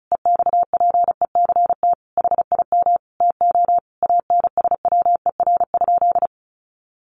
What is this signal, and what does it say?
Morse code, heard as a keyed tone, about 34 wpm, 700 Hz. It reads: EXPECT HIM TO ANSWER?
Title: Sentences from Top 1100 Words 35wpm